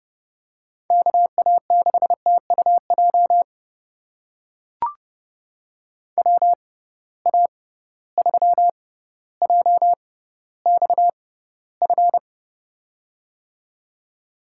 Morse code, heard as a keyed tone, about 30 wpm, spaced as for 15 wpm, 700 Hz. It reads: KA6TUJ WA3JXF
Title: USA Call Signs - 6 Characters - 6x Character Spacing - 30wpm